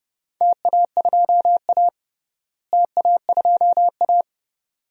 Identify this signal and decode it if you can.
Morse code, heard as a keyed tone, about 32 wpm, 700 Hz. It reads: TA2A TA2A
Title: DXCC-100 Call Signs - 4 Characters - Repeated 2x - 30wpm